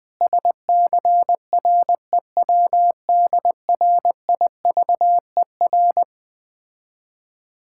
Morse code, heard as a keyed tone, about 20 wpm, 700 Hz. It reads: SCREWDRIVER
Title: Difficult Words with Hints - 20wpm